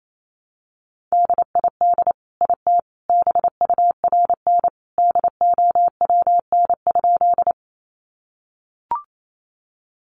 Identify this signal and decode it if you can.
Morse code, heard as a keyed tone, about 28 wpm, 700 Hz. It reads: DID IT BURN DOWN?
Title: Sentences from Top 1300 Words 28wpm